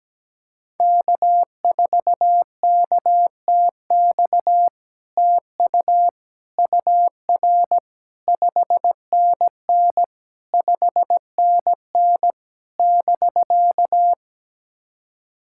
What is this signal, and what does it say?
Morse code, heard as a keyed tone, about 17 wpm, 700 Hz. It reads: K4KTX TU UR 5NN 5NN <BK>
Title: SOTA QSO Practice 17wpm